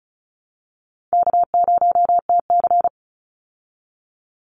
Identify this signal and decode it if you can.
Morse code, heard as a keyed tone, about 34 wpm, 700 Hz. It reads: K0TC
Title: Call Signs Repeated 2x - 35wpm